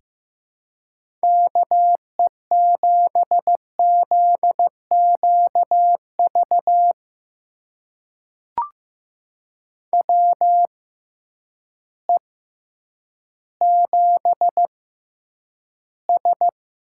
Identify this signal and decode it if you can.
Morse code, heard as a keyed tone, about 15 wpm, spaced as for 9 wpm, 700 Hz. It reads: KE7ZQV WE7S
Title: USA Call Signs - 6x Character Spacing - 15wpm